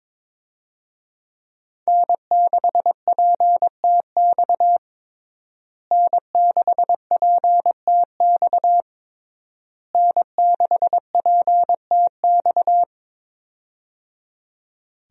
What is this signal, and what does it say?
Morse code, heard as a keyed tone, about 22 wpm, 700 Hz. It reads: N6PTX N6PTX N6PTX
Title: USA Call Signs - 5 Characters - Repeated 3x - 22wpm